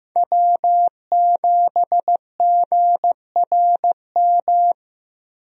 Morse code, heard as a keyed tone, about 15 wpm, 700 Hz. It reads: W7GRM
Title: USA Call Signs - 5 Characters - Repeated 2x - 15wpm